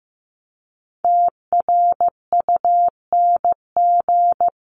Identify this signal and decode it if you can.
Morse code, heard as a keyed tone, about 15 wpm, 700 Hz. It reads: TRUNG
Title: CWT Exchange 15wpm